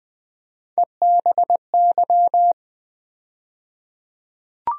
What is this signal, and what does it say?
Morse code, heard as a keyed tone, about 20 wpm, 700 Hz. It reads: EBY
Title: USA Call Sign Postfixes - 3 Letters 20wpm